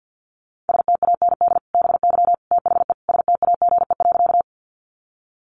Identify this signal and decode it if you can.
Morse code, heard as a keyed tone, about 50 wpm, 700 Hz. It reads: STAND BY THE STAGE.
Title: Sentences from Top 1200 Words 50wpm